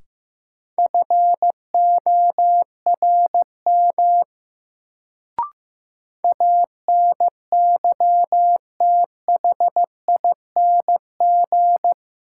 Morse code, heard as a keyed tone, about 15 wpm, 700 Hz. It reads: FORM ANYTHING
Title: Sets of 4 Words - New Word Drill - 15wpm